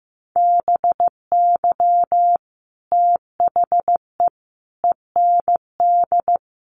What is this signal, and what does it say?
Morse code, heard as a keyed tone, about 15 wpm, 700 Hz. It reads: BY THE END
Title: Sets of 5 Words 15wpm